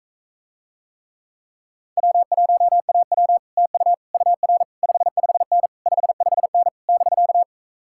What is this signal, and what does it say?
Morse code, heard as a keyed tone, about 45 wpm, 700 Hz. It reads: W1AW TU UR 55N 55N <BK>